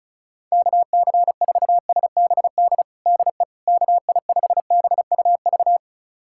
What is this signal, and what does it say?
Morse code, heard as a keyed tone, about 34 wpm, 700 Hz. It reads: KC4SBD DE KI5BUV
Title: USA Callsign DE Callsign - 6 Characters - 35wpm